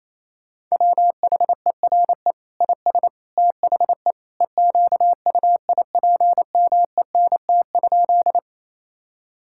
Morse code, heard as a keyed tone, about 27 wpm, 700 Hz. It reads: WHERE IS THE EQUIPMENT?
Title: Sentences from Top 1000 Words - Review Part 5 - 28wpm